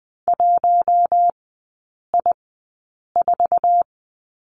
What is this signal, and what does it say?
Morse code, heard as a keyed tone, about 20 wpm, spaced as for 8 wpm, 700 Hz. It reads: 1I4
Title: 047 Lesson 9e - Number 4 - Three Characters 20wpm